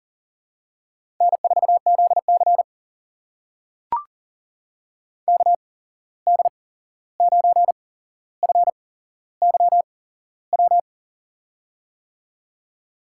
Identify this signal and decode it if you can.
Morse code, heard as a keyed tone, about 42 wpm, spaced as for 15 wpm, 700 Hz. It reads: N4ZC KD9FYW
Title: USA Call Signs - 8x Character Spacing - 40wpm